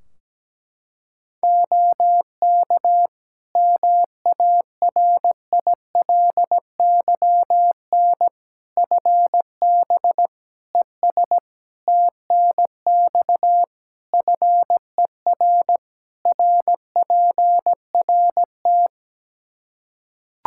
Morse code, heard as a keyed tone, about 17 wpm, 700 Hz. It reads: OK MARILYN FB ES TNX FER RPRT
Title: Practice QSO - Exchange 2 - Info - 17wpm